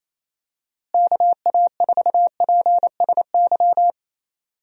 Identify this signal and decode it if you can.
Morse code, heard as a keyed tone, about 28 wpm, 700 Hz. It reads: KA4PSY
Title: USA Call Signs - 6 Characters - Repeated 2x - 28wpm